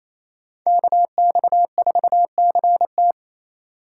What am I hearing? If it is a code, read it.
Morse code, heard as a keyed tone, about 28 wpm, 700 Hz. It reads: KX4CT
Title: USA Call Signs - 5 Characters - 4x Character Spacing - 28wpm